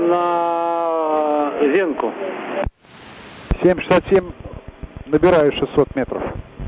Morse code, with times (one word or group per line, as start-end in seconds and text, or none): none